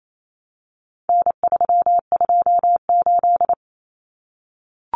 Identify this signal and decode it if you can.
Morse code, heard as a keyed tone, about 28 wpm, 700 Hz. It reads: N328
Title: Common Cut Numbers - 4-Digit 28wpm